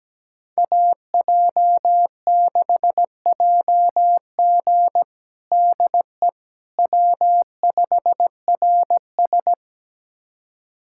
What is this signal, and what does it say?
Morse code, heard as a keyed tone, about 17 wpm, 700 Hz. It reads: AJ6JG DE W5RS
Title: USA Callsign DE Callsign 17wpm